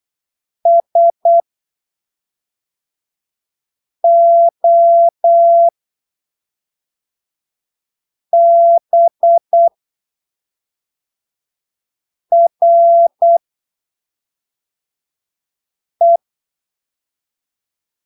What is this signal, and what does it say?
Morse code, heard as a keyed tone, about 8 wpm, spaced as for 3 wpm, 680 Hz. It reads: SOBRE
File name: arquivos / texto / texto 24 08ppm